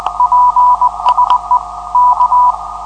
Morse code, heard as a keyed tone, about 20 wpm, 990 Hz, with noise. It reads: PSE K
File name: 472khz qso web / 472khz 14dec2012 / dk7fc-1 1try